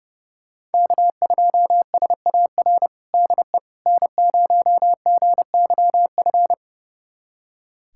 Morse code, heard as a keyed tone, about 32 wpm, 700 Hz. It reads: K2SAR DE N0GYF
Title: USA Callsign DE Callsign - 5 Characters - 30wpm